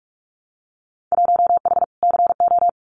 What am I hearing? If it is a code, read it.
Morse code, heard as a keyed tone, about 46 wpm, 700 Hz. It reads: 1H CO
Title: Winter Field Day Exchanges 45wpm